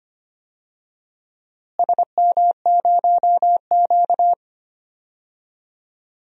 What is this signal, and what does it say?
Morse code, heard as a keyed tone, about 25 wpm, 700 Hz. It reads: SM0Q
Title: DXCC-100 Call Signs - 4 Characters - Repeated 3x - 25wpm